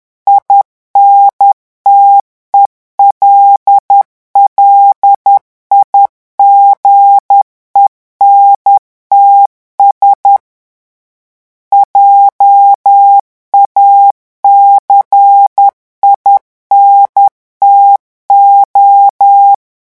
Morse code, heard as a keyed tone, about 10 wpm, 800 Hz. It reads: INTELLIGENTS JACINTO
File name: revision9 citations humoristiques vitesse 10 mots minute